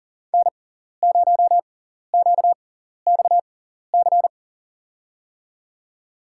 Morse code, heard as a keyed tone, about 40 wpm, spaced as for 14 wpm, 700 Hz. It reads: N0QXC
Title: USA Call Signs - 5 Characters - 6x Character Spacing - 40wpm